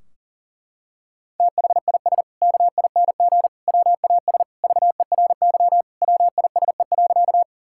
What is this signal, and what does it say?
Morse code, heard as a keyed tone, about 40 wpm, 700 Hz. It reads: THIS KING WAS VERY WISE.
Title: Sentences from Top 1600 Words 40wpm